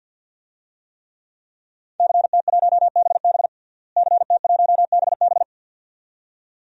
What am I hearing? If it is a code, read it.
Morse code, heard as a keyed tone, about 47 wpm, 700 Hz. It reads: CT1BB CT1BB
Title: DXCC-100 Call Signs - 5 Characters - Repeated 2x - 50wpm